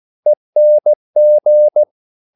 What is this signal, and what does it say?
Morse code, heard as a keyed tone, about 16 wpm, 600 Hz. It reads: ENG
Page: Korte danske ord 16wpm